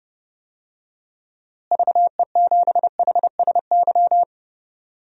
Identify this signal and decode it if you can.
Morse code, heard as a keyed tone, about 29 wpm, 700 Hz. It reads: VE7HSY
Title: DXCC-100 Call Signs - 6 Characters - Repeated 2x - 30wpm